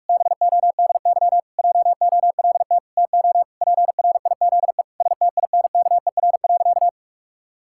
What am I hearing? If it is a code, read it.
Morse code, heard as a keyed tone, about 46 wpm, 700 Hz. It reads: BODY JOLT TO PRIZE STINKER.
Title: Pangram Challenge v2 - 45wpm